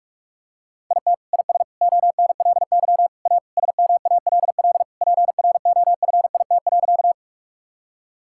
Morse code, heard as a keyed tone, about 46 wpm, 700 Hz. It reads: IT IS ONLY A SMALL PROFIT.
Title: Sentences from Top 1000 Words - Review Part 5 - 45wpm